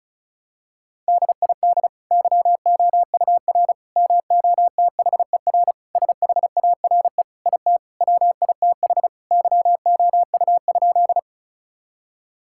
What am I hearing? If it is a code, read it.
Morse code, heard as a keyed tone, about 36 wpm, 700 Hz. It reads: DID YOUR MOTHER SHARE IT WITH YOU?
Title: Sentences from Top 1000 Words - Review Part 6 - 35wpm